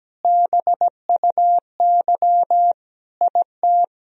Morse code, heard as a keyed tone, about 17 wpm, 700 Hz. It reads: BUY IT
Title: Sentences from Top 1000 Words - Review Part 4 - 17wpm